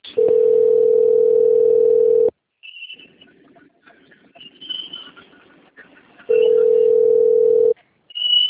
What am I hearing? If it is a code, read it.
Ringback tone (North America), 440 + 480 Hz.
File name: out- - - - - .18619